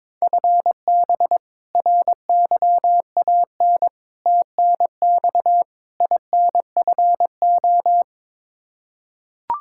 Morse code, heard as a keyed tone, about 21 wpm, 700 Hz. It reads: FB RYAN TNX INFO
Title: Practice QSO - Exchange 3 - More Info - 22wpm